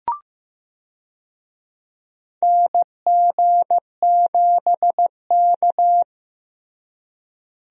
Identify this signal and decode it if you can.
Morse code, heard as a keyed tone, about 15 wpm, 700 Hz. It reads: NG7K
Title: USA Call Signs - 4 Characters - Repeated 2x - 15wpm